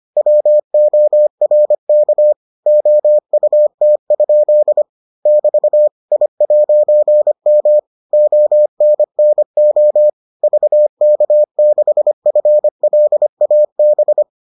Morse code, heard as a keyed tone, about 25 wpm, 600 Hz. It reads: WORK OUT? <BT> I'M ONNO VK6FLAB